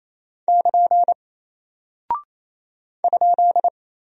Morse code, heard as a keyed tone, about 28 wpm, 700 Hz. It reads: <KN> ?
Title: Single Character (Letter:Number:Punctuation:Prosign) Alt 28wpm